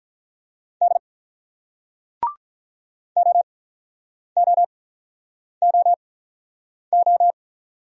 Morse code, heard as a keyed tone, about 49 wpm, spaced as for 9 wpm, 700 Hz. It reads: D OOOO